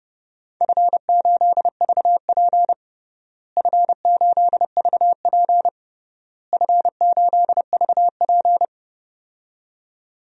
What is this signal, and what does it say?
Morse code, heard as a keyed tone, about 30 wpm, 700 Hz. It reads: F8VP F8VP F8VP
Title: DXCC-100 Call Signs - 4 Characters - Repeated 3x - 30wpm